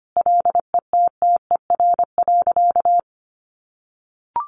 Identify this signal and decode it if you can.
Morse code, heard as a keyed tone, about 25 wpm, 700 Hz. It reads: LETTER.
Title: A Sound Of Thunder 25wpm